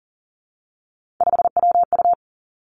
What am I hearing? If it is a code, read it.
Morse code, heard as a keyed tone, about 40 wpm, 700 Hz. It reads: 5WU
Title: Three Challenging Letter-Number Combinations 40wpm